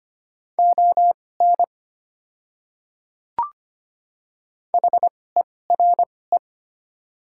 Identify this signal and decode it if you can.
Morse code, heard as a keyed tone, about 25 wpm, spaced as for 18 wpm, 690 Hz. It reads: ON HERE
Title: 2x Character Spacing - Top 200 Words - 25wpm